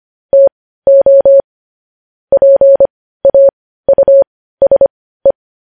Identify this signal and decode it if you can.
Morse code, heard as a keyed tone, about 25 wpm, spaced as for 15 wpm, 550 Hz. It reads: TO PAUSE
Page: news headlines in morse code at 15 wpm